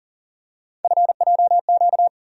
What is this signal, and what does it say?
Morse code, heard as a keyed tone, about 40 wpm, 700 Hz. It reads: FJQ
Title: USA Call Sign Postfixes - 3 Letters 40wpm